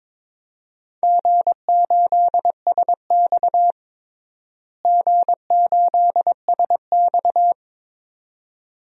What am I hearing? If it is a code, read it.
Morse code, heard as a keyed tone, about 22 wpm, 700 Hz. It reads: G8SX G8SX
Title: DXCC-100 Call Signs - 4 Characters - Repeated 2x - 22wpm